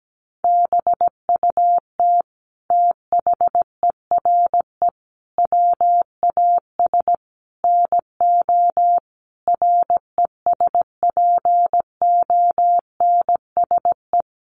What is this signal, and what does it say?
Morse code, heard as a keyed tone, about 17 wpm, 700 Hz. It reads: BUT THERE WAS NO RESPONSE
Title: Sentences from Top 1400 Words 17wpm